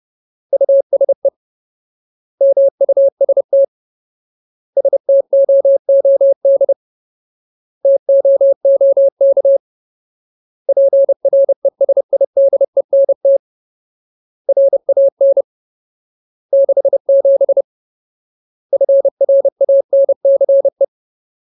Morse code, heard as a keyed tone, about 30 wpm, 550 Hz. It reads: USE MUST STOOD TOOK PRESIDENT RAN 67 FRANCE
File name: Ultimate Word List / 30WPM Ultimate